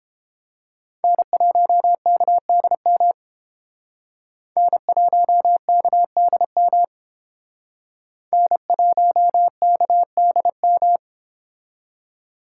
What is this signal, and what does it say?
Morse code, heard as a keyed tone, about 29 wpm, 700 Hz. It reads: N1KDM N1KDM N1KDM